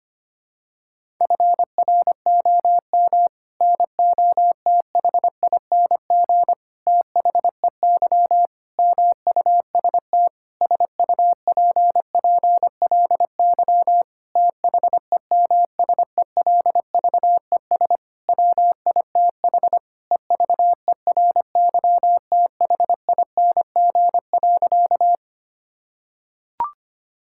Morse code, heard as a keyed tone, about 25 wpm, 700 Hz. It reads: FROM NOTHING THEY MUST SUPPLY THEMSELVES WITH EVERYTHING.
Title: Sentences from Top 1000 Words - Review Part 5 - 25wpm